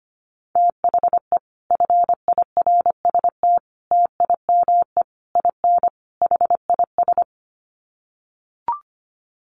Morse code, heard as a keyed tone, about 25 wpm, 700 Hz. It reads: THE FIRST TIME IN HIS
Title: Sets of 5 Words Encore2 25wpm